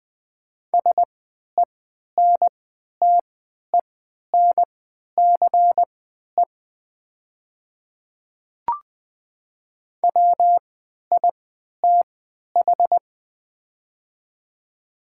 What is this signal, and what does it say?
Morse code, heard as a keyed tone, about 20 wpm, spaced as for 11 wpm, 700 Hz. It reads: SENTENCE WITH
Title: 3x Character Spacing - Top 500 Words - 20wpm